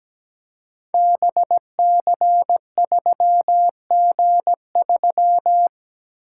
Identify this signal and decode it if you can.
Morse code, heard as a keyed tone, about 17 wpm, 700 Hz. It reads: BC3G3
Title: Code Groups - 5 Letter:Numbers - 17wpm